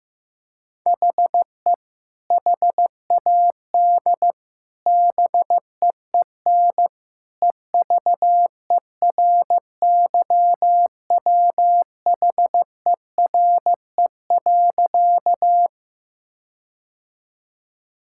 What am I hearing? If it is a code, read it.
Morse code, heard as a keyed tone, about 15 wpm, 700 Hz. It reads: HE HAD BEEN EVERYWHERE.
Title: Sentences from Top 1600 Words 15wpm